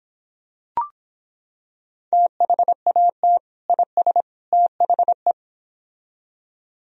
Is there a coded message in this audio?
Yes: Morse code from 2.1 to 5.3 s, heard as a keyed tone, about 26 wpm, 700 Hz. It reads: THAT IS THE